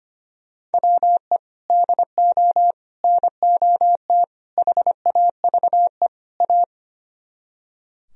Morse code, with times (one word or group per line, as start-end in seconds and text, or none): 0.7-1.4 WE
1.7-2.7 DO
3.0-4.2 NOT
4.6-6.1 HAVE
6.4-6.6 A